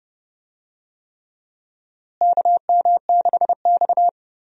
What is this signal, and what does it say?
Morse code, heard as a keyed tone, about 31 wpm, 700 Hz. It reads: KM6X
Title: USA Call Signs - 4 Characters - Repeated 3x - 30wpm